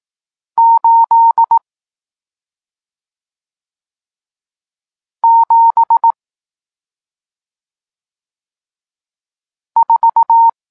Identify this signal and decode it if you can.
Morse code, heard as a keyed tone, about 18 wpm, 920 Hz. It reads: 8 7 4
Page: Tekst 40 – Nauka znaków przestankowych